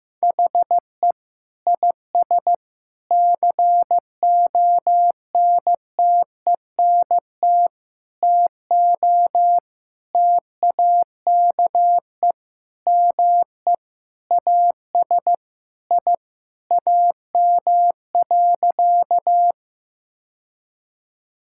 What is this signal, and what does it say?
Morse code, heard as a keyed tone, about 15 wpm, 700 Hz. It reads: HE IS CONTENT TO TAKE ME AS I AM.
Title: Sentences from Top 1000 Words - Review Part 6 - 15wpm